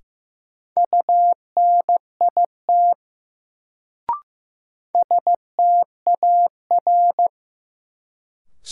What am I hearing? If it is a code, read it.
Morse code, heard as a keyed tone, about 15 wpm, 700 Hz. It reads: UNIT STAR
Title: Sentences from Top 1000 Words - Review Part 2 - New Word Drill - 15wpm